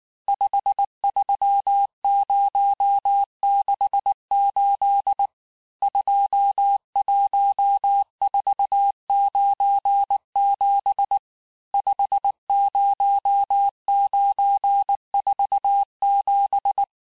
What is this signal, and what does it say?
Morse code, heard as a keyed tone, about 19 wpm, 800 Hz. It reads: 53068 21497 50947
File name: cours-du-mercredi / audios / cours-869-texte-CHIFFRE-vitesse-18-MPM